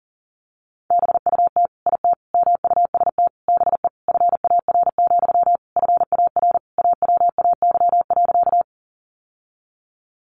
Morse code, heard as a keyed tone, about 40 wpm, 700 Hz. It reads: BUT IT MUST BE FAR, FAR AWAY.
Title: Sentences from Top 600 Words 40wpm